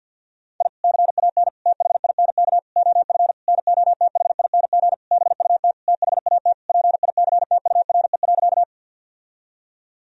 Morse code, heard as a keyed tone, about 51 wpm, 700 Hz. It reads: I CAN THINK OF NOTHING BUT THAT PICTURE.
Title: Sentences from Top 1000 Words - Review Part 1 - 50wpm